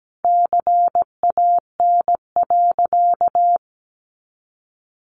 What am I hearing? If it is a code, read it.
Morse code, heard as a keyed tone, about 17 wpm, 700 Hz. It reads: CAN.
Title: Sentences from Top 1100 Words 17wpm